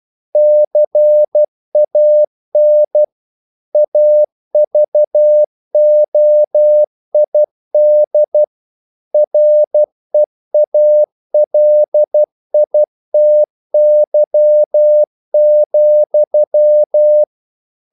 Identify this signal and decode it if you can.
Morse code, heard as a keyed tone, about 12 wpm, 600 Hz. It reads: CAN AVOID REALITY,
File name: CW / Practice / EnglishQuotations0112